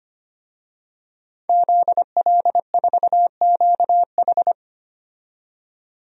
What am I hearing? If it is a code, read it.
Morse code, heard as a keyed tone, about 25 wpm, 700 Hz. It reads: ZL4QH
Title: DXCC-100 Call Signs - 5 Characters - 25wpm